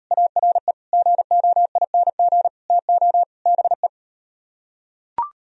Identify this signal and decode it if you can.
Morse code, heard as a keyed tone, about 38 wpm, 700 Hz. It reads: ARE GOING TO BE